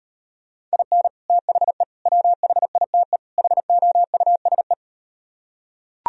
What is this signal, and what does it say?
Morse code, heard as a keyed tone, about 38 wpm, 700 Hz. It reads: IN THE WHITE HOUSE